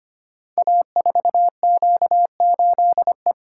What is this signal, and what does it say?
Morse code, heard as a keyed tone, about 25 wpm, 700 Hz. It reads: A4Q8E
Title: Code Groups - 5 Letter:Numbers - 25wpm